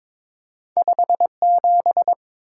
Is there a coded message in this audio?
Yes: Morse code from 0.8 to 2.1 s, heard as a keyed tone, about 29 wpm, 700 Hz. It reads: 57